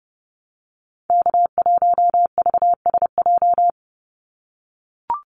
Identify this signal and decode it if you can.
Morse code, heard as a keyed tone, about 30 wpm, 700 Hz. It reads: K1VSJ
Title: Call Signs - 30wpm